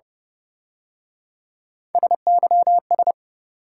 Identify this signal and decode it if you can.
Morse code, heard as a keyed tone, about 30 wpm, 700 Hz. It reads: SYS